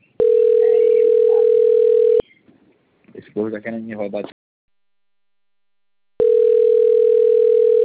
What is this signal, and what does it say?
Ringback tone (North America), 440 + 480 Hz, with speech.
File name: rg-1600-unknown- - - .40807